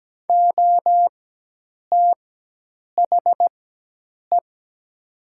Morse code, heard as a keyed tone, about 17 wpm, spaced as for 8 wpm, 700 Hz. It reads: OTHE
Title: 4x Character Spacing - Top 300 Words - 17wpm